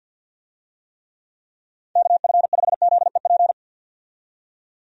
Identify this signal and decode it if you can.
Morse code, heard as a keyed tone, about 51 wpm, 700 Hz. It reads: KF5ZEP
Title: USA Call Signs - 6 Characters - 50wpm